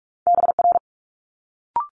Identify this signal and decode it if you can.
Morse code, heard as a keyed tone, about 46 wpm, 700 Hz. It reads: BR
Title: Multi-letter Phonemes Quick Fire 45wpm